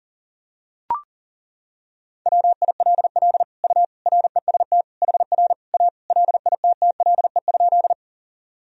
Morse code, heard as a keyed tone, about 40 wpm, 700 Hz. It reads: WILL U REST HR A LITTLE?
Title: Sentences from Top 500 Words and Abbreviations 40wpm